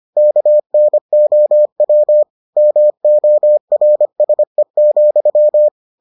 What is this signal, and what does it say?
Morse code, heard as a keyed tone, about 25 wpm, 600 Hz. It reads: KNOW MORSE,